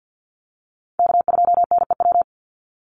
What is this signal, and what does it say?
Morse code, heard as a keyed tone, about 50 wpm, 700 Hz. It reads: K2NEW
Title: USA Call Signs 50wpm